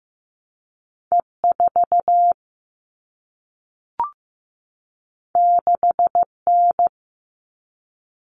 Morse code, heard as a keyed tone, about 15 wpm, 700 Hz. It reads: E4 6N
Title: All Cut Numbers - 2-Digit 15wpm